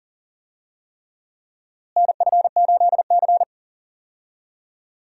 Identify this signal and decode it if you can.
Morse code, heard as a keyed tone, about 37 wpm, 700 Hz. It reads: NF8C
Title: USA Call Signs - 4 Characters - Repeated 3x - 40wpm